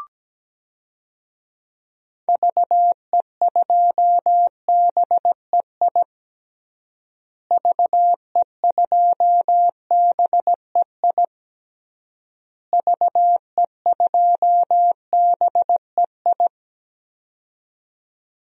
Morse code, heard as a keyed tone, about 17 wpm, 700 Hz. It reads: VE2BEI VE2BEI VE2BEI
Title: DXCC-100 Call Signs - 6 Characters - Repeated 3x - 17wpm